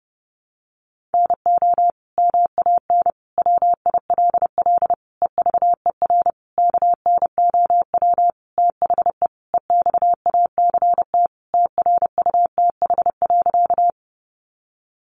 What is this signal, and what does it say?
Morse code, heard as a keyed tone, about 30 wpm, 700 Hz. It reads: NO MAN WILL EVER KNOW THE EXACT TRUTH.
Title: Sentences from Top 1300 Words 30wpm